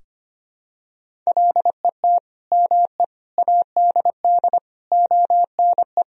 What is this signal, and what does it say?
Morse code, heard as a keyed tone, about 25 wpm, 700 Hz. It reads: LET ME ADD ONE
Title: Sentences from Top 1000 Words - Review Part 1 - 25wpm